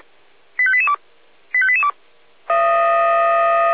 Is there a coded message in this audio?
Yes: five-tone selective call (ZVEI) from 0.6 to 1.9 s, heard as quick runs of five tones, tones of about 70 ms. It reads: 85882 85882